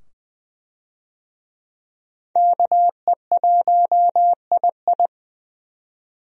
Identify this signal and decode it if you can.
Morse code, heard as a keyed tone, about 20 wpm, 700 Hz. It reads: KE1II
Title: USA Call Signs - 5 Characters - 20wpm